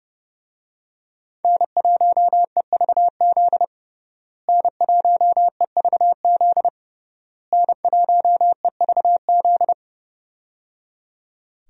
Morse code, heard as a keyed tone, about 31 wpm, 700 Hz. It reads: N1EVZ N1EVZ N1EVZ
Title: USA Call Signs - 5 Characters - Repeated 3x - 30wpm